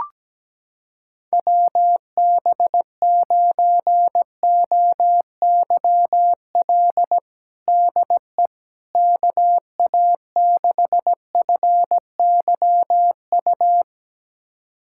Morse code, heard as a keyed tone, about 17 wpm, 700 Hz. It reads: WB9OYL DE KA6FYU
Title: USA Callsign DE Callsign - 6 Characters - 17wpm